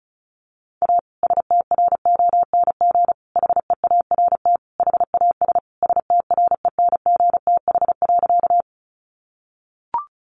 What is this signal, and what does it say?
Morse code, heard as a keyed tone, about 34 wpm, 700 Hz. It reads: A STRONG HEART HAS STRENGTH.
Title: Sentences from Top 1300 Words 35wpm